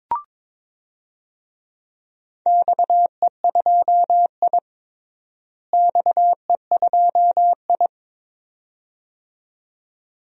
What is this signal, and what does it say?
Morse code, heard as a keyed tone, about 22 wpm, 700 Hz. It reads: XE2I XE2I
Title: DXCC-100 Call Signs - 4 Characters - Repeated 2x - 22wpm